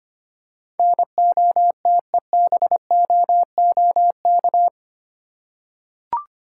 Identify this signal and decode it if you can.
Morse code, heard as a keyed tone, about 25 wpm, 700 Hz. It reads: NOTEBOOK
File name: practice / 25wpm / Easy Words with Hints - 25wpm